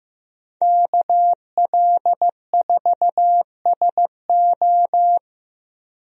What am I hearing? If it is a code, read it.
Morse code, heard as a keyed tone, about 16 wpm, 700 Hz. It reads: KL4SO
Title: USA Call Signs - 5 Characters - Repeated 3x - 15wpm